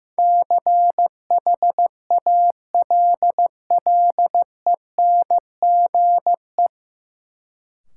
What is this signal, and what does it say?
Morse code, heard as a keyed tone, about 15 wpm, 700 Hz. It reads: CHALLENGE
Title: Sentences from Top 1000 Words - Review Part 7 - New Word Drill - 15wpm